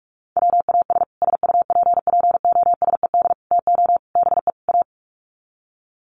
Morse code, heard as a keyed tone, about 44 wpm, 700 Hz. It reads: WAS SUPPOSED TO BE A
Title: Sets of 5 Words Encore 45wpm